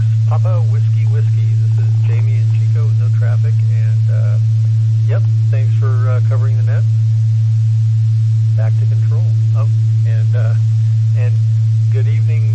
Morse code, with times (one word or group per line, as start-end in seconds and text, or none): none